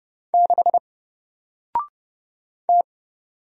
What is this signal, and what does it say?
Morse code, heard as a keyed tone, about 30 wpm, 700 Hz. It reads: 6 T
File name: Single-Letter-Number-30wpm